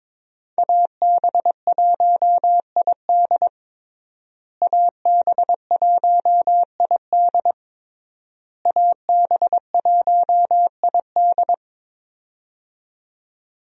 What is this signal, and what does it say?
Morse code, heard as a keyed tone, about 21 wpm, 700 Hz. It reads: AB1ID AB1ID AB1ID
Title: USA Call Signs - 5 Characters - Repeated 3x - 22wpm